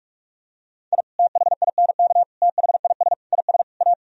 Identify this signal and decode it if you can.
Morse code, heard as a keyed tone, about 43 wpm, 700 Hz. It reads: I THINK THIS IS A